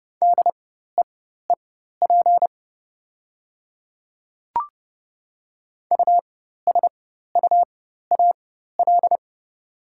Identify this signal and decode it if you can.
Morse code, heard as a keyed tone, about 30 wpm, spaced as for 14 wpm, 700 Hz. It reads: DEEP USUAL
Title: 4x Character Spacing - Top 500 Words - 30wpm